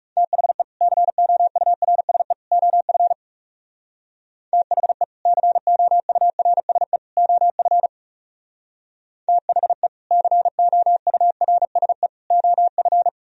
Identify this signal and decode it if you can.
Morse code, heard as a keyed tone, about 38 wpm, 700 Hz. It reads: THE COURSE OF THE COURSE OF THE COURSE OF